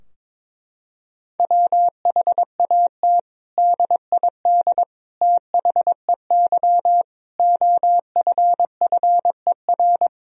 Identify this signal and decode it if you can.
Morse code, heard as a keyed tone, about 22 wpm, 700 Hz. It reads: WHAT DID THEY OFFER
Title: Sentences from Top 1000 Words - Review Part 4 - 22wpm